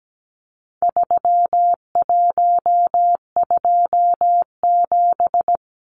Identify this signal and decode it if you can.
Morse code, heard as a keyed tone, about 17 wpm, 700 Hz. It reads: 3127
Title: Numbers - 4-Digit 17wpm